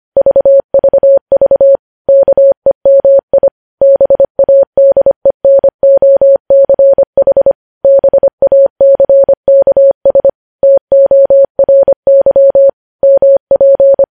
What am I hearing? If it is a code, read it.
Morse code, heard as a keyed tone, about 25 wpm, 550 Hz. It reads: VVV KEMI BADENOCH BACKS TORY MP
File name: podcast / News headlines-1x-25x25